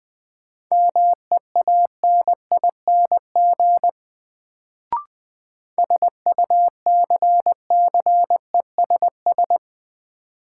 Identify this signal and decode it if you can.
Morse code, heard as a keyed tone, about 20 wpm, 700 Hz. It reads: MEANING SUCCESS
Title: Sentences from Top 900 Words - New Word Drill - 20wpm